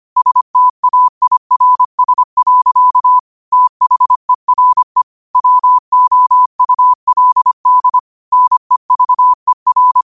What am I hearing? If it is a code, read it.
Morse code, heard as a keyed tone, about 25 wpm, 1000 Hz. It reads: STAIRS. THERE WOULD NEVER
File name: qotdfiles / 25wpm